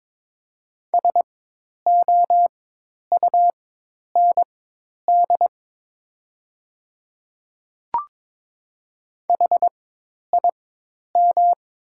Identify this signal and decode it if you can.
Morse code, heard as a keyed tone, about 22 wpm, spaced as for 10 wpm, 700 Hz. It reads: SOUND HIM
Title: 4x Character Spacing - Top 100 Words - 22wpm